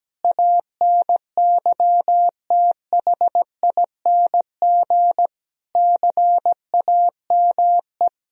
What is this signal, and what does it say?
Morse code, heard as a keyed tone, about 17 wpm, 700 Hz. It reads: ANYTHING CAME
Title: Sentences from Top 1000 Words - Review Part 3 - 17wpm